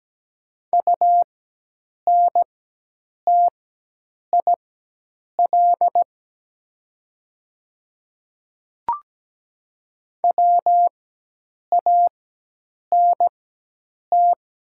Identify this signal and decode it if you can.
Morse code, heard as a keyed tone, about 17 wpm, spaced as for 8 wpm, 700 Hz. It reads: UNTIL WANT
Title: 4x Character Spacing - Top 300 Words - 17wpm